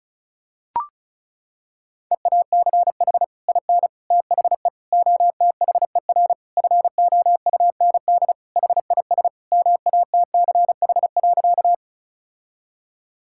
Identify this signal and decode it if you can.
Morse code, heard as a keyed tone, about 34 wpm, 700 Hz. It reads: EACH IN THE OTHER FOUND HIS MATCH.
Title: Sentences from Top 1600 Words 35wpm